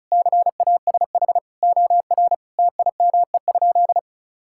Morse code, heard as a keyed tone, about 34 wpm, 700 Hz. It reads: CASH OR TIME?
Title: Sentences from Top 1000 Words - Review Part 5 - 35wpm